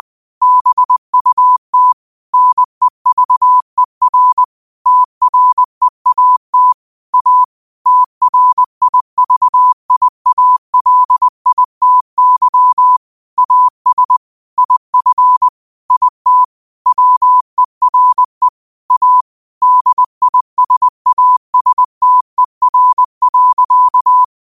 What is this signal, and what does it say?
Morse code, heard as a keyed tone, about 20 wpm, 1000 Hz. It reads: BUT NEVER TREAT A TRIVIALITY AS IF IT WERE A DISASTER.
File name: qotdfiles / 20wpm